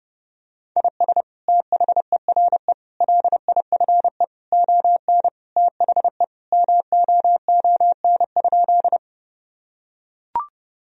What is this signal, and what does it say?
Morse code, heard as a keyed tone, about 32 wpm, 700 Hz. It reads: IS THERE LIFE ON THE MOON?
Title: Sentences from Top 1000 Words - Review Part 3 - 30wpm